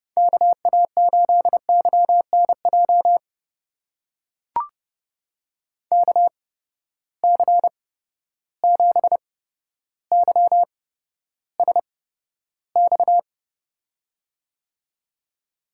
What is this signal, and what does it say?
Morse code, heard as a keyed tone, about 30 wpm, spaced as for 13 wpm, 700 Hz. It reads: KA8YNJ KC7YSX
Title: USA Call Signs - 6 Characters - 8x Character Spacing - 30wpm